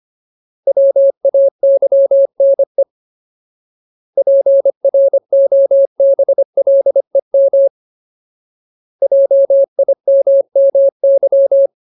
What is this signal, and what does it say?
Morse code, heard as a keyed tone, about 25 wpm, 550 Hz. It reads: WAYNE PROBLEM JIMMY
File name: Ultimate Word List / 25WPM Ultimate